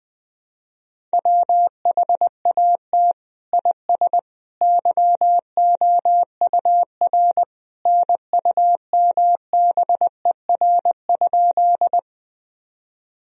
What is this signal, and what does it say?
Morse code, heard as a keyed tone, about 21 wpm, 700 Hz. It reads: WHAT IS YOUR NUMBER?
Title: Sentences from Top 1000 Words - Review Part 1 - 20wpm